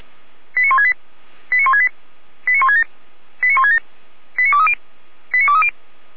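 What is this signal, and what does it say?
Five-tone selective call (ZVEI), heard as quick runs of five tones, tones of about 80 ms. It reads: 89168 89168 89167 89167 89230 89230
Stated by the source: ZVEI five-tone call 89121, received by SDR.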